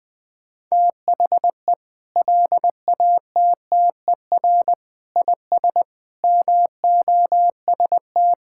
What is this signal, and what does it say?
Morse code, heard as a keyed tone, about 20 wpm, 700 Hz. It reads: THE LATTER IS MOST
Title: Sentences from Top 1600 Words 20wpm